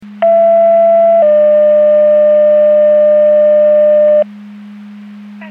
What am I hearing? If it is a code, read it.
Two-tone page, heard as a one-second tone, then a three-second tone, about 670 then 600 Hz. It is heard from 0.2 to 4.2 s.
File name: Portage 154.385 Paging